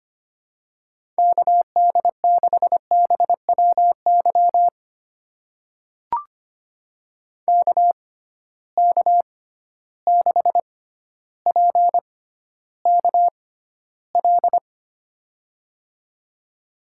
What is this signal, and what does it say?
Morse code, heard as a keyed tone, about 25 wpm, spaced as for 13 wpm, 700 Hz. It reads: KD6BWY KK6PKL
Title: USA Call Signs - 6 Characters - 6x Character Spacing - 25wpm